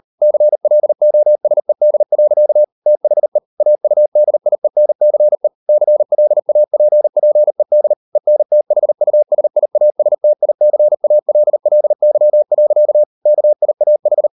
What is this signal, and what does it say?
Morse code, heard as a keyed tone, about 38 wpm, 600 Hz. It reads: CLOSED. THE AUDIENCE CLAPPED ENTHUSIASTICALLY. KIAH